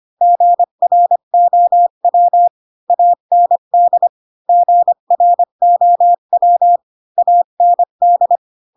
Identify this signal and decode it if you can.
Morse code, heard as a keyed tone, about 25 wpm, 700 Hz. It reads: GROW AND GROW AND
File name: Pinocchio / Pinocchio ch3 20 25 ik1hge